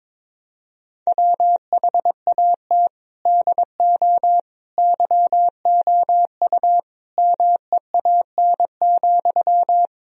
Morse code, heard as a keyed tone, about 22 wpm, 700 Hz. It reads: WHAT DO YOU MEAN,
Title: Sentences from Top 1000 Words - Review Part 4 - 22wpm